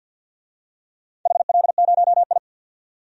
Morse code, heard as a keyed tone, about 47 wpm, 700 Hz. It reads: HL0I
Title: DXCC-100 Call Signs - 4 Characters - 50wpm